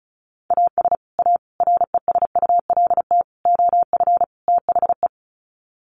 Morse code, heard as a keyed tone, about 34 wpm, 700 Hz. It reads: AS A RESULT OF THE
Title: Sets of 5 Words 35wpm